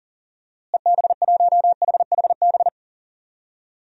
Morse code, heard as a keyed tone, about 40 wpm, 700 Hz. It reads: EB1HHB
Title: DXCC-100 Call Signs - 6 Characters - Repeated 2x - 40wpm